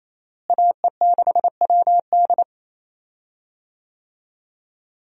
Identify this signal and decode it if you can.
Morse code, heard as a keyed tone, about 28 wpm, 700 Hz. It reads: AE6WD
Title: USA Call Signs - 5 Characters - Repeated 3x - 28wpm